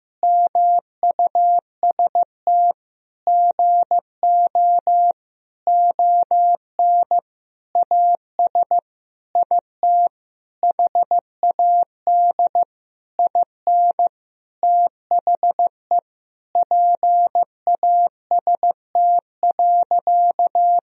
Morse code, heard as a keyed tone, about 16 wpm, 700 Hz. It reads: MUST GO ON AS IT HAD IN THE PAST.
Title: Sentences from Top 1000 Words - Review Part 3 - 15wpm